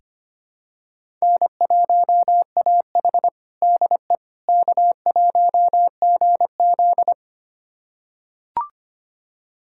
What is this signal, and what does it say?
Morse code, heard as a keyed tone, about 25 wpm, 700 Hz. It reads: N1AH DE K1GZ
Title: USA Callsign DE Callsign - 4 Characters - 25wpm